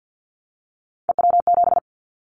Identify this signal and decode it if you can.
Morse code, heard as a keyed tone, about 50 wpm, 690 Hz. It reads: EW7
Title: Callsign Prefixes for Top 100 DXCC - 50wpm